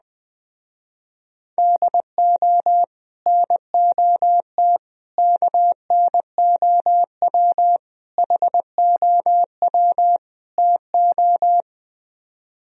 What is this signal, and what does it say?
Morse code, heard as a keyed tone, about 20 wpm, 700 Hz. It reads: DO NOT KNOW HOW TO
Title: Speed-Racing - Sets of 5 Words 17wpm